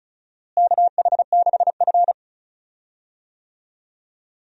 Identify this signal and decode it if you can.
Morse code, heard as a keyed tone, about 36 wpm, 700 Hz. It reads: KH6F
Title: USA Call Signs - 4 Characters - Repeated 2x - 35wpm